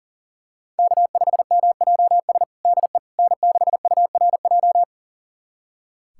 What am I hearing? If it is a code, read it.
Morse code, heard as a keyed tone, about 40 wpm, 700 Hz. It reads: K5MJS DE N6URJ
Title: USA Callsign DE Callsign - 5 Characters - 40wpm